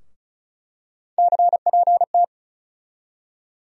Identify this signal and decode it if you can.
Morse code, heard as a keyed tone, about 36 wpm, 700 Hz. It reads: CPT
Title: USA Call Sign Postfixes - 3 Letters 35wpm